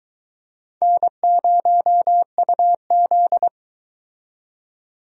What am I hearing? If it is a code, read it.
Morse code, heard as a keyed tone, about 22 wpm, 700 Hz. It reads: N0UZ